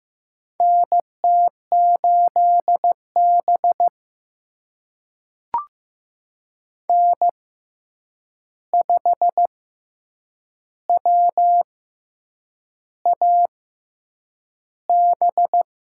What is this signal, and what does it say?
Morse code, heard as a keyed tone, about 15 wpm, spaced as for 7 wpm, 700 Hz. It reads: NT8B N5WAB
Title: USA Call Signs - 6x Character Spacing - 15wpm